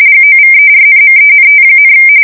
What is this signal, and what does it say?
RTTY, about 45.45 baud. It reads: GOVERNMENT,
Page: 170 Shift 2125/2295